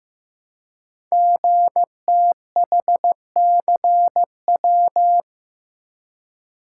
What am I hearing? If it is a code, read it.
Morse code, heard as a keyed tone, about 15 wpm, 700 Hz. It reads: GTHCW
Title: Code Groups - 5 Letters - 15wpm